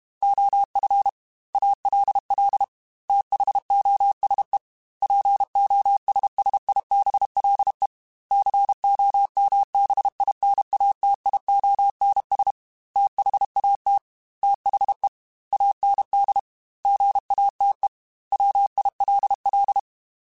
Morse code, heard as a keyed tone, about 32 wpm, 800 Hz. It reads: OF ALL THOSE POSSIBLE COMBINATIONS THAT THE AND GATE WILL
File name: Technical Subjects / Technical Subjects-mp3s / AND Gate @30